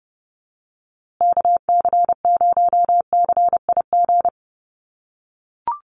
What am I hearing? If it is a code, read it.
Morse code, heard as a keyed tone, about 30 wpm, 700 Hz. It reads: KC0CIG
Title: USA Call Signs Repeated 2x 30wpm